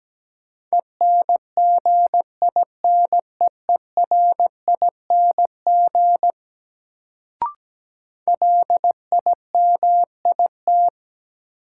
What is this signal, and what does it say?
Morse code, heard as a keyed tone, about 17 wpm, 700 Hz. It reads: ENGINEERING LIMIT
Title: Sentences from Top 1500 Words - New Word Drill Alt - 17wpm